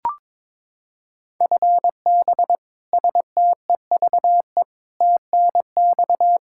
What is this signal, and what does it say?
Morse code, heard as a keyed tone, about 22 wpm, 700 Hz. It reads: FB STEVE TNX
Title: Practice QSO - Exchange 3 - More Info - 22wpm